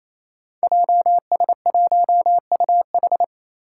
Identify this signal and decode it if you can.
Morse code, heard as a keyed tone, about 27 wpm, 700 Hz. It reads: JS1UH
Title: Five Challenging Letter-Number Combinations 28wpm